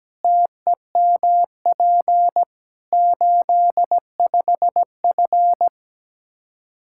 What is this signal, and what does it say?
Morse code, heard as a keyed tone, about 17 wpm, 700 Hz. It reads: TEMP 85F
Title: Practice QSO - Exchange 2 - Info - 17wpm